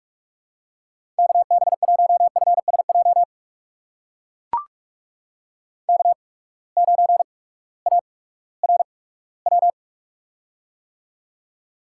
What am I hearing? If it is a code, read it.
Morse code, heard as a keyed tone, about 45 wpm, spaced as for 21 wpm, 700 Hz. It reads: KB1FSJ K9ARW
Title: USA Call Signs - 8x Character Spacing - 45wpm